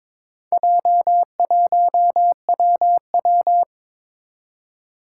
Morse code, heard as a keyed tone, about 23 wpm, 700 Hz. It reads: J1WW
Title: Four Challenging Letter-Number Combinations 22wpm